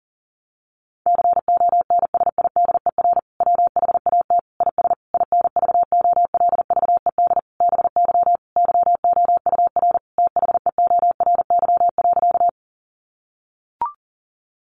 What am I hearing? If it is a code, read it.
Morse code, heard as a keyed tone, about 40 wpm, 700 Hz. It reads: CONSIDER WHAT IS INVOLVED BY YOUR THEORY.
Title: Sentences from Top 800 Words 40wpm